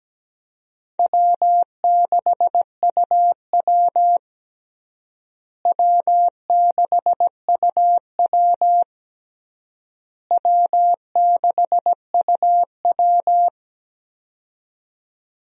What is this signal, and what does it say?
Morse code, heard as a keyed tone, about 17 wpm, 700 Hz. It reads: W6UW W6UW W6UW
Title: USA Call Signs - 4 Characters - Repeated 3x - 17wpm